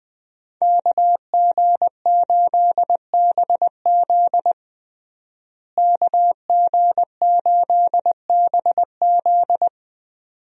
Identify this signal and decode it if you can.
Morse code, heard as a keyed tone, about 20 wpm, 700 Hz. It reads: KG8BZ KG8BZ
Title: USA Call Signs - 5 Characters - Repeated 2x - 20wpm